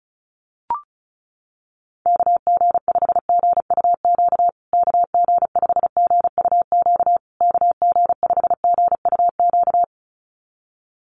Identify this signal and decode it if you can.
Morse code, heard as a keyed tone, about 34 wpm, 700 Hz. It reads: KG5GUQ KG5GUQ KG5GUQ
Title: USA Call Signs Repeated 3x 35wpm